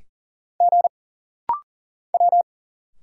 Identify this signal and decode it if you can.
Morse code, heard as a keyed tone, about 40 wpm, 700 Hz. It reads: G W
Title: Single Letters - Rapid-Fire 40wpm